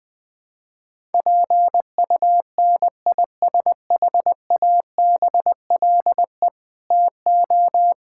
Morse code, heard as a keyed tone, about 20 wpm, 700 Hz. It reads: PUNISHABLE TO
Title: Sentences with 7-Word Phrases - 20wpm